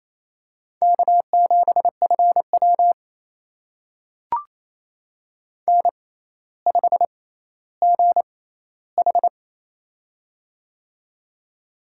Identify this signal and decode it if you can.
Morse code, heard as a keyed tone, about 27 wpm, spaced as for 14 wpm, 700 Hz. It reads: K7FW N5GH
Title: USA Call Signs - 4 Characters - 6x Character Spacing - 28wpm